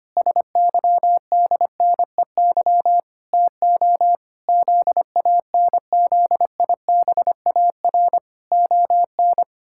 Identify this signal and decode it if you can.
Morse code, heard as a keyed tone, about 25 wpm, 700 Hz. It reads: SYDNEY TO ZANZIBAR ON
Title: Pangram Challenge v2 - 25wpm